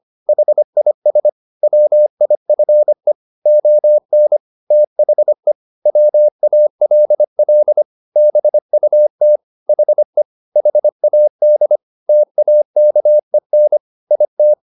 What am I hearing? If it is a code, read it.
Morse code, heard as a keyed tone, about 25 wpm, 600 Hz. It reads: HIS WIFE ON THE WALL BUT HE HAD TAKEN IT
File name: the old man and the sea part 1 25wpm